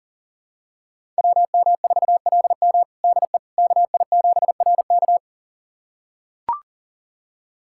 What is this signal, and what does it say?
Morse code, heard as a keyed tone, about 40 wpm, 700 Hz. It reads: WM4LM DE KI7RK
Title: USA Callsign DE Callsign - 5 Characters - 40wpm